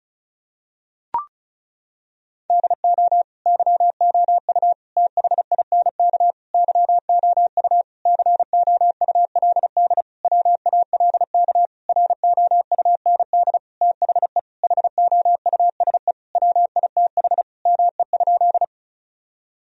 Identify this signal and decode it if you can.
Morse code, heard as a keyed tone, about 34 wpm, 700 Hz. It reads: DO YOU THINK YOU COULD WALK ROUND THE HOUSE WITH ME?
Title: Sentences from Top 1000 Words - Review Part 1 - 35wpm